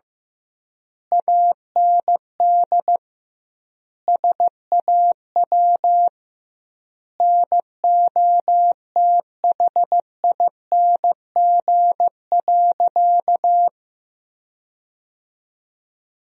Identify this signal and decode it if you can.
Morse code, heard as a keyed tone, about 15 wpm, 700 Hz. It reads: AND SAW NOTHING.
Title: 2x Word-Spacing - Sentences from Top 500 Words - 15wpm